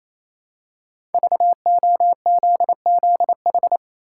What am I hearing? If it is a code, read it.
Morse code, heard as a keyed tone, about 28 wpm, 700 Hz. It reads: VOZZH
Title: Code Groups - 5 Letter:Numbers - 28wpm